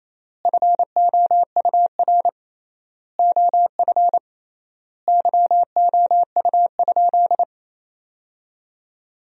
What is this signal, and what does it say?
Morse code, heard as a keyed tone, about 28 wpm, 700 Hz. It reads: FOUR OF YOU?
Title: 3x Word-Spacing - Sentences from Top 300 Words - 28wpm